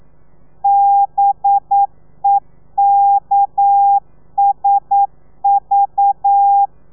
Morse code, heard as a keyed tone, about 9 wpm, 800 Hz. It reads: BEKSV
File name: telegrafia / i2tfj / Lez4 40